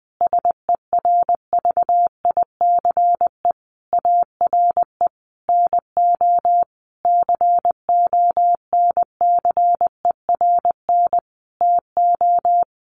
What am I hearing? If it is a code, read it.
Morse code, heard as a keyed tone, about 20 wpm, 700 Hz. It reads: SERVICE ARE NO CONCERN TO
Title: Sentences from Top 1200 Words 20wpm